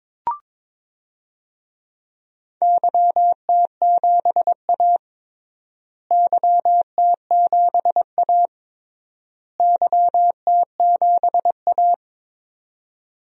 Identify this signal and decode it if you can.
Morse code, heard as a keyed tone, about 22 wpm, 700 Hz. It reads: YT7A YT7A YT7A
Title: DXCC-100 Call Signs - 4 Characters - Repeated 3x - 22wpm